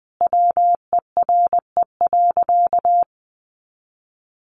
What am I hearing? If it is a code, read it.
Morse code, heard as a keyed tone, about 19 wpm, 700 Hz. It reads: WERE.
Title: Sentences from Top 1000 Words 20wpm